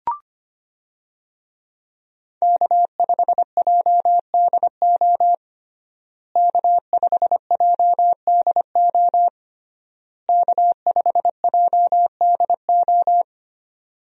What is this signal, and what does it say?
Morse code, heard as a keyed tone, about 25 wpm, 700 Hz. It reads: K5JDO K5JDO K5JDO
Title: USA Call Signs - 5 Characters - Repeated 3x - 25wpm